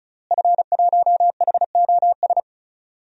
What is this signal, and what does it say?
Morse code, heard as a keyed tone, about 34 wpm, 700 Hz. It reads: F1HOS
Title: DXCC-100 Call Signs - 5 Characters - Repeated 2x - 35wpm